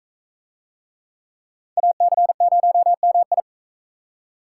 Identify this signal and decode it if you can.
Morse code, heard as a keyed tone, about 42 wpm, 700 Hz. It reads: AC0MI